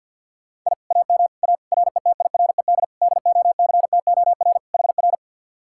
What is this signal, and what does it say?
Morse code, heard as a keyed tone, about 67 wpm, spaced as for 58 wpm, 700 Hz. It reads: I AM A RETIRED DOCTOR HR
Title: Speed-Racing Common QSO Phrases - 50wpm